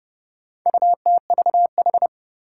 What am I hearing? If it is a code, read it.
Morse code, heard as a keyed tone, about 30 wpm, 700 Hz. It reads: UTVH
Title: Four Challenging Letter-Number Combinations 30wpm